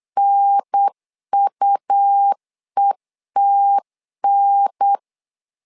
Morse code, heard as a keyed tone, about 8 wpm, 800 Hz. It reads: NUETN
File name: LEZIONI / 3 MWRO / 3 MWR 8wpm 2444